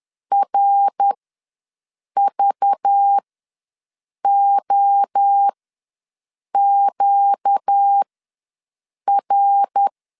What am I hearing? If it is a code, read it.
Morse code, heard as a keyed tone, about 10 wpm, spaced as for 6 wpm, 800 Hz. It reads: RVOQR
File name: LEZIONI / 5 LBVC / 5 LBV ws 10wpm 3645